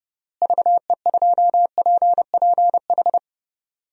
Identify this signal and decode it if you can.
Morse code, heard as a keyed tone, about 30 wpm, 700 Hz. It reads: VE2PPH
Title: DXCC-100 Call Signs - 6 Characters - Repeated 3x - 30wpm